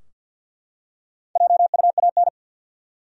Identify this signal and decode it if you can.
Morse code, heard as a keyed tone, about 51 wpm, 700 Hz. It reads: JUAN
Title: USA Common Names - Max 4 Letters - 50wpm